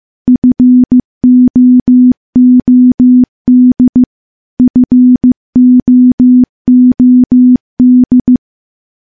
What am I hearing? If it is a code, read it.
Morse code, heard as a keyed tone, about 15 wpm, 250 Hz. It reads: FOOD FOOD
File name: morse / morse-training-15WPM